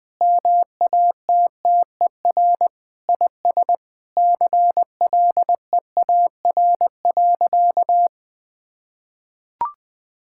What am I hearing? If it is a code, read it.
Morse code, heard as a keyed tone, about 20 wpm, 700 Hz. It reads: MATTER IS CLEAR.
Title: Sentences with 6-Word Phrases - 20wpm